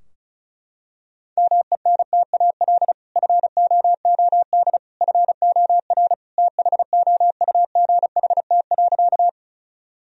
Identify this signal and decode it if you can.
Morse code, heard as a keyed tone, about 34 wpm, 700 Hz. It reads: MENTAL FOOD FOR THOUGHT.
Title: Sentences from Top 1000 Words - Review Part 4 - 35wpm